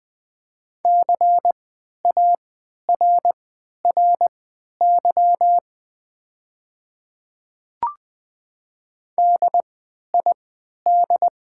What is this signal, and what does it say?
Morse code, heard as a keyed tone, about 20 wpm, spaced as for 11 wpm, 700 Hz. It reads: CARRY DID
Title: 3x Character Spacing - Top 500 Words - 20wpm